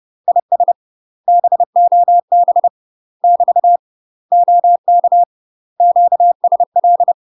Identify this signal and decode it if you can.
Morse code, heard as a keyed tone, about 30 wpm, 700 Hz. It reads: IS BOB <BT> OK QSL
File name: UK MASTER QSOs TRAINING / 30WPM / UK MASTER QSOs 0049